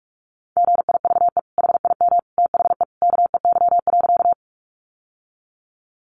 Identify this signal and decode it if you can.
Morse code, heard as a keyed tone, about 44 wpm, 700 Hz. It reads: GIVE HIM THE KEY.
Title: Sentences from Top 600 Words 45wpm